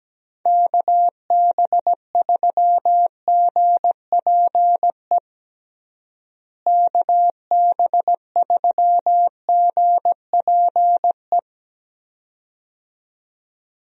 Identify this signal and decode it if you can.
Morse code, heard as a keyed tone, about 17 wpm, 700 Hz. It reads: KB3GPE KB3GPE
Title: USA Call Signs - 6 Characters - Repeated 2x - 17wpm